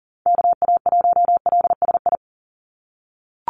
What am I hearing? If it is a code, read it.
Morse code, heard as a keyed tone, about 37 wpm, 690 Hz. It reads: KA1LSI
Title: USA Call Signs 40wpm